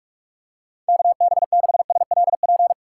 Morse code, heard as a keyed tone, about 43 wpm, 700 Hz. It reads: KB6SLP
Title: USA Call Signs - 6 Characters - 2x Character Spacing - 45wpm